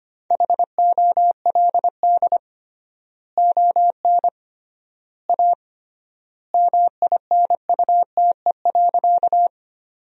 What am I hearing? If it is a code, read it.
Morse code, heard as a keyed tone, about 25 wpm, 700 Hz. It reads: HOLD ON A MINUTE.
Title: 3x Word-Spacing - Sentences from Top 500 Words - 25wpm